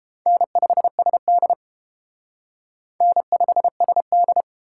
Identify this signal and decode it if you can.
Morse code, heard as a keyed tone, about 31 wpm, 700 Hz. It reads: N5SD N5SD